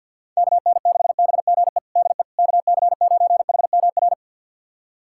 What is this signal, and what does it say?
Morse code, heard as a keyed tone, about 53 wpm, 700 Hz. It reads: KN6BGE DE KC9HMR
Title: USA Callsign DE Callsign 50wpm